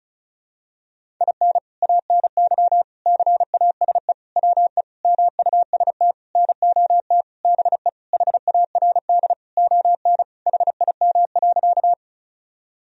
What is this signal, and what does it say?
Morse code, heard as a keyed tone, about 36 wpm, 700 Hz. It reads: IN ANY CASE WE MUST NOT BE HARD ON HIM.
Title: Sentences from Top 1000 Words - Review Part 1 - 35wpm